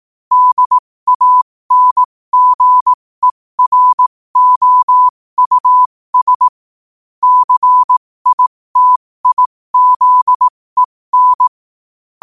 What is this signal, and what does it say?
Morse code, heard as a keyed tone, about 18 wpm, spaced as for 15 wpm, 1000 Hz. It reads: DANGEROUS CITIZEN
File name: qotdfiles / 15wpm